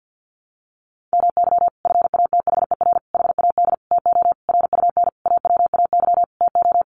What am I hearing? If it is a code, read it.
Morse code, heard as a keyed tone, about 50 wpm, 700 Hz. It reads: MY FATHER HAD TO RUN AWAY TO
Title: Sentences from Top 1100 Words 50wpm